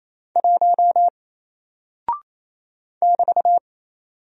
Morse code, heard as a keyed tone, about 28 wpm, 700 Hz. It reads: 1 <BT>
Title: Single Character (Letter:Number:Punctuation:Prosign) Alt 28wpm